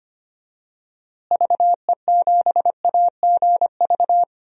Morse code, heard as a keyed tone, about 25 wpm, 700 Hz. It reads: VE7AGV
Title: DXCC-100 Call Signs - 6 Characters - Repeated 3x - 25wpm